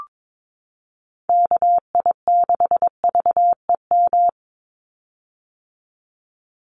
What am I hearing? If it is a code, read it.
Morse code, heard as a keyed tone, about 22 wpm, 700 Hz. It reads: KI6VEM
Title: USA Call Signs 22wpm